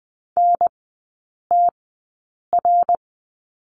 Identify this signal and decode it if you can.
Morse code, heard as a keyed tone, about 20 wpm, spaced as for 8 wpm, 700 Hz. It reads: NTR
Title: 114 Lesson 19e - Letter w - Three Characters 20wpm